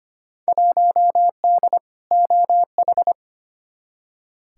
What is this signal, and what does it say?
Morse code, heard as a keyed tone, about 25 wpm, 700 Hz. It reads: 1D OH
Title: Field Day Exchanges - 25wpm